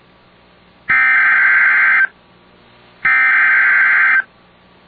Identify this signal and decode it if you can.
Emergency Alert System (SAME) tones (header bursts), heard as harsh buzzing bursts. It is heard from 0.9 to 4.2 s.